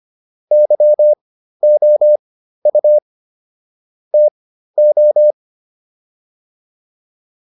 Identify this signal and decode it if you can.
Morse code, heard as a keyed tone, about 25 wpm, spaced as for 13 wpm, 600 Hz. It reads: YOU TO